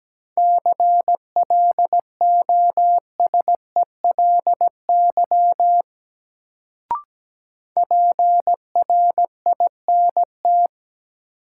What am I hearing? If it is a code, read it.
Morse code, heard as a keyed tone, about 17 wpm, 700 Hz. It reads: CLOSELY PRINT
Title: Sentences from Top 1400 Words - New Word Drill - 17wpm